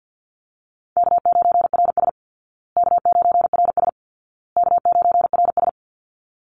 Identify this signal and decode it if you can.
Morse code, heard as a keyed tone, about 50 wpm, 700 Hz. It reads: K9RS K9RS K9RS
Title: Easy Call Signs Repeated 3x - 50wpm